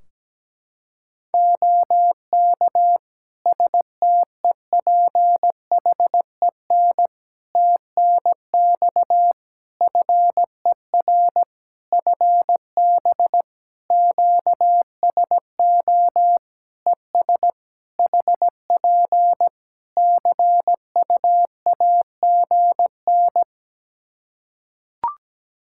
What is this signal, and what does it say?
Morse code, heard as a keyed tone, about 17 wpm, 700 Hz. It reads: OK STEPHEN TNX FER FB QSO ES HP CUAGN
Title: K1USN SST QSO Practice 17wpm